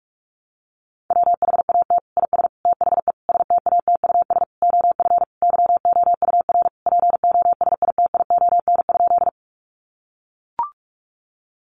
Morse code, heard as a keyed tone, about 44 wpm, 700 Hz. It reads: WHAT IS THE STATUS OF YOUR POSITION?
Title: Sentences from Top 1400 Words 45wpm